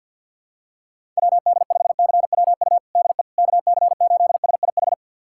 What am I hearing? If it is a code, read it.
Morse code, heard as a keyed tone, about 50 wpm, 700 Hz. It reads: WD5CWU DE KC8SIH
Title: USA Callsign DE Callsign 50wpm